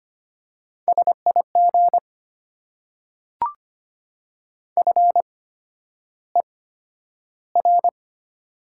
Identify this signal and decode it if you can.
Morse code, heard as a keyed tone, about 25 wpm, spaced as for 11 wpm, 700 Hz. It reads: SIG FER
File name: Top 90 QSO Elements - 8x Character Spacing - 25wpm